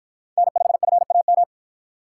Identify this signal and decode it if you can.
Morse code, heard as a keyed tone, about 50 wpm, 700 Hz. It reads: N5FAM